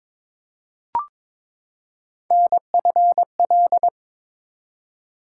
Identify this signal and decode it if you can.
Morse code, heard as a keyed tone, about 21 wpm, 700 Hz. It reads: NFL
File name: practice / 22wpm / ARRL Sections v2 - 22wpm